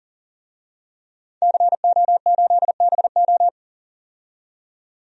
Morse code, heard as a keyed tone, about 39 wpm, 700 Hz. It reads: CO8BO
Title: DXCC-100 Call Signs - 5 Characters - 40wpm